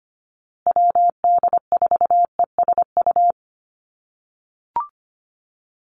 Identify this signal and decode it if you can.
Morse code, heard as a keyed tone, about 25 wpm, 700 Hz. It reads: WD4ESU
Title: USA Call Signs Repeated 3x 25wpm